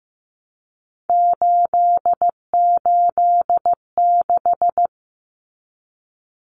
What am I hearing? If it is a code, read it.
Morse code, heard as a keyed tone, about 15 wpm, 700 Hz. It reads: 886
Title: Numbers - 3-Digit 15wpm